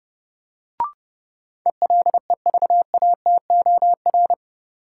Morse code, heard as a keyed tone, about 30 wpm, 700 Hz. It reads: ELEVATOR
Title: Sentences from Top 1600 Words - New Word Drill - 30wpm